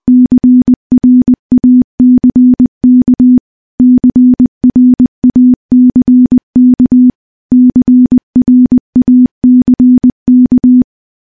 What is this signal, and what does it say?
Morse code, heard as a keyed tone, about 21 wpm, 250 Hz. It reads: CRACK CRACK CRACK
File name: morse-training-20WPM